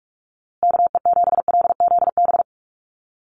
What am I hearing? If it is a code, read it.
Morse code, heard as a keyed tone, about 44 wpm, 700 Hz. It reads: KE7LZB
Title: USA Call Signs Repeated 2x 45wpm